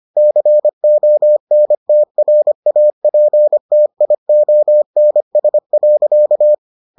Morse code, heard as a keyed tone, about 25 wpm, 600 Hz. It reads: CONTRAPTIONS.